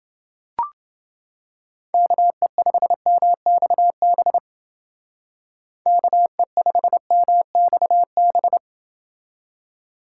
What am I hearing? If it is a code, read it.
Morse code, heard as a keyed tone, about 30 wpm, 700 Hz. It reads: KE5MXB KE5MXB